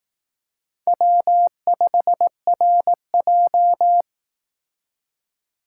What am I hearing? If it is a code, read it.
Morse code, heard as a keyed tone, about 18 wpm, 700 Hz. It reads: W5RJ
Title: Speed Racing - USA 1x2 Call Signs 15wpm